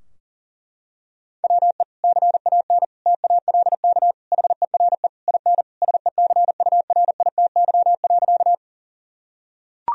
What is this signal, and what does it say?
Morse code, heard as a keyed tone, about 37 wpm, 700 Hz. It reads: WE CAN TALK HERE IN SECURITY.
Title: Sentences from Top 1000 Words - Review Part 4 - 40wpm